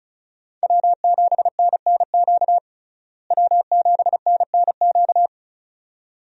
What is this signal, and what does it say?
Morse code, heard as a keyed tone, about 34 wpm, 700 Hz. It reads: W7NNQ W7NNQ
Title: USA Call Signs - 5 Characters - Repeated 2x - 35wpm